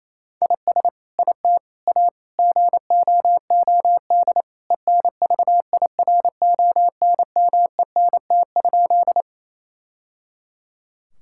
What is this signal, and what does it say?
Morse code, heard as a keyed tone, about 28 wpm, 700 Hz. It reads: IS IT A GOOD ENVIRONMENT?
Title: Sentences from Top 1000 Words - Review Part 5 - 28wpm